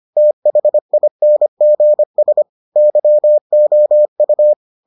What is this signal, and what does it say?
Morse code, heard as a keyed tone, about 25 wpm, 600 Hz. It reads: THINGS YOU